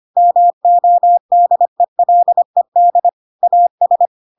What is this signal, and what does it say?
Morse code, heard as a keyed tone, about 25 wpm, 700 Hz. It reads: MODELED AS